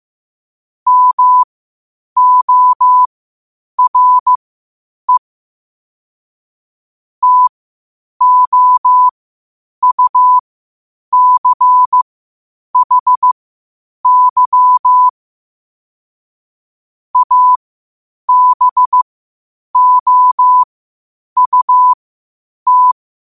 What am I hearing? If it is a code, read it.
Morse code, heard as a keyed tone, about 15 wpm, spaced as for 8 wpm, 1000 Hz. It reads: MORE TOUCHY ABOUT
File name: qotdfiles / 8wpm